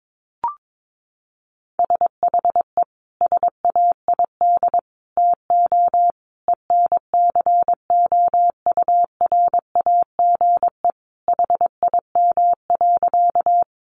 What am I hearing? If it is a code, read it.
Morse code, heard as a keyed tone, about 22 wpm, 700 Hz. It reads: SHE SAID TO ENCOURAGE HIM.
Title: Sentences from Top 1400 Words 22wpm